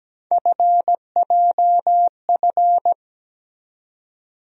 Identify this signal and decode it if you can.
Morse code, heard as a keyed tone, about 17 wpm, 700 Hz. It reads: FJF
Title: USA Call Sign Postfixes - 3 Letters 17wpm